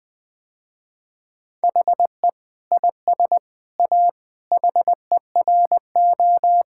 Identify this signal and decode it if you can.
Morse code, heard as a keyed tone, about 20 wpm, 700 Hz. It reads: HE IS A HERO
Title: Sentences with 6-Word Phrases - Spoken Beginning - 20wpm